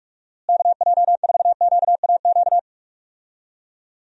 Morse code, heard as a keyed tone, about 45 wpm, 700 Hz. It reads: KJ4QAQ
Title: USA Call Signs - 6 Characters - Repeated 2x - 45wpm